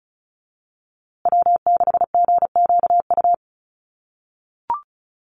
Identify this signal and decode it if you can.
Morse code, heard as a keyed tone, about 34 wpm, 700 Hz. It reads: W6GQU
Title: USA Call Signs Repeated 2x 35wpm